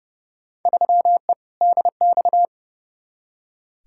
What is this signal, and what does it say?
Morse code, heard as a keyed tone, about 32 wpm, 700 Hz. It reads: 3E DX
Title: Field Day Exchanges Encore - 30wpm